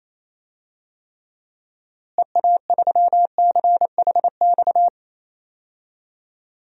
Morse code, heard as a keyed tone, about 28 wpm, 700 Hz. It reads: EA3CHX
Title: DXCC-100 Call Signs - 6 Characters - Repeated 2x - 28wpm